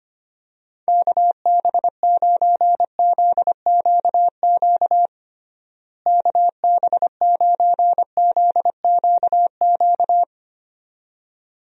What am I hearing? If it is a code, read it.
Morse code, heard as a keyed tone, about 25 wpm, 700 Hz. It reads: KB9ZQQ KB9ZQQ
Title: USA Call Signs - 6 Characters - Repeated 2x - 25wpm